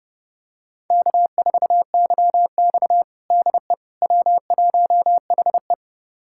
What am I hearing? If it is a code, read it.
Morse code, heard as a keyed tone, about 30 wpm, 700 Hz. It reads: K4YX DE W1HE
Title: USA Callsign DE Callsign - 4 Characters - 30wpm